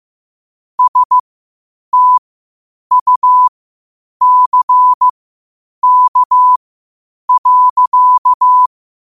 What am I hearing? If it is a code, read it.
Morse code, heard as a keyed tone, about 15 wpm, spaced as for 8 wpm, 1000 Hz. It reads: STUCK.
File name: qotdfiles / 8wpm